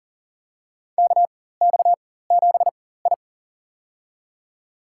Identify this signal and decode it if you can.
Morse code, heard as a keyed tone, about 40 wpm, spaced as for 19 wpm, 700 Hz. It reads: KX7I
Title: USA Call Signs - 4 Characters - 4x Character Spacing - 40wpm